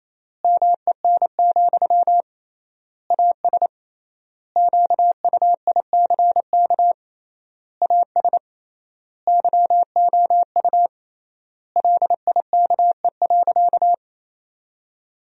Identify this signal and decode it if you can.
Morse code, heard as a keyed tone, about 27 wpm, 700 Hz. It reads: MEN, AS QUICK AS YOU LIKE.
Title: 3x Word-Spacing - Sentences from Top 500 Words - 28wpm